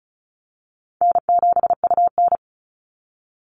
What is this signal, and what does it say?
Morse code, heard as a keyed tone, about 34 wpm, 700 Hz. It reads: N7UN
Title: NA SOTA Activator Callsigns 35wpm